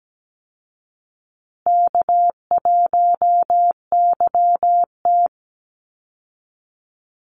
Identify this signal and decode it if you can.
Morse code, heard as a keyed tone, about 17 wpm, 700 Hz. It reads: K1YT
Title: Easy Call Signs Repeated 3x - 17wpm